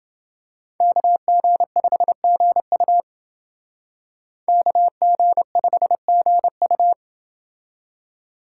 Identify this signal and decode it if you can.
Morse code, heard as a keyed tone, about 29 wpm, 700 Hz. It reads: KG5GU KG5GU